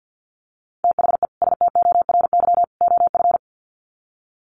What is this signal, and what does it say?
Morse code, heard as a keyed tone, about 50 wpm, 700 Hz. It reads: THE STORY OF
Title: Sets of 3 Words 50wpm